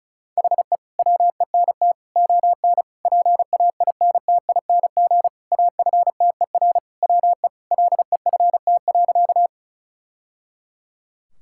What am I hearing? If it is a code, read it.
Morse code, heard as a keyed tone, about 36 wpm, 700 Hz. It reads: HE WENT ON PAINTING AFTER WE LEFT.
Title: Sentences from Top 1000 Words - Review Part 6 - 35wpm